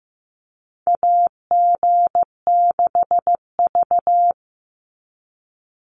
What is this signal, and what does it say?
Morse code, heard as a keyed tone, about 15 wpm, 700 Hz. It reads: AG6V
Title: NA SOTA Chaser Callsigns 15wpm